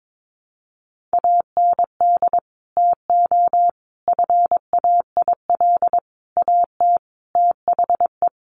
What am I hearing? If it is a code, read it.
Morse code, heard as a keyed tone, about 22 wpm, 700 Hz. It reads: AND TO FAIL AT THE
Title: Sentences from Top 1100 Words 22wpm